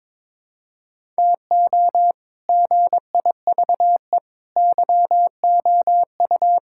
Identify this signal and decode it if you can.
Morse code, heard as a keyed tone, about 22 wpm, 700 Hz. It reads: TO GIVE YOU
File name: practice / 22wpm / Sets of 7 Words - 22wpm